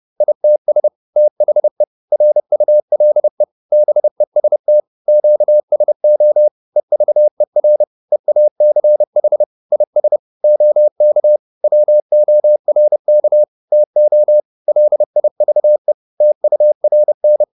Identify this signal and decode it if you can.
Morse code, heard as a keyed tone, about 30 wpm, 600 Hz. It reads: ITS THE RULE BEST QSO EVER EACH IS OK WORK TO LIVE TURN